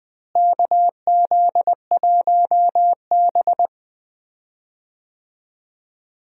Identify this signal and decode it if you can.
Morse code, heard as a keyed tone, about 20 wpm, 700 Hz. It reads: KZ1B
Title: USA Call Signs - 4 Characters - Repeated 3x - 20wpm